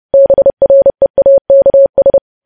Morse code, heard as a keyed tone, about 30 wpm, 550 Hz. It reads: BREAKS
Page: news headlines in morse code at 30 wpm